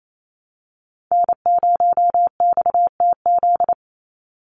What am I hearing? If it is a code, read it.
Morse code, heard as a keyed tone, about 28 wpm, 700 Hz. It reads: N0XTZ
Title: USA Call Signs 28wpm